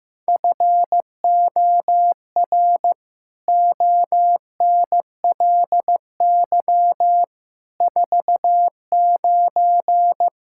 Morse code, heard as a keyed tone, about 14 wpm, 700 Hz. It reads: FOR ONLY 49
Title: Pangram Challenge v2 - 15wpm